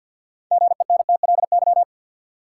Morse code, heard as a keyed tone, about 51 wpm, 700 Hz. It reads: GENTLY
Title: Sentences from Top 1500 Words - New Word Drill Alt - 50wpm